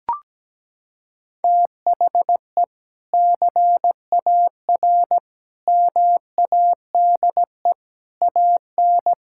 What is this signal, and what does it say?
Morse code, heard as a keyed tone, about 17 wpm, 700 Hz. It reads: THE CAR MADE AN
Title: Sentences from Top 1000 Words - Review Part 6 - 17wpm